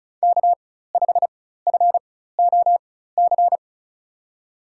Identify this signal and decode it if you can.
Morse code, heard as a keyed tone, about 34 wpm, spaced as for 16 wpm, 700 Hz. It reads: K5FOC
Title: USA Call Signs - 5 Characters - 4x Character Spacing - 35wpm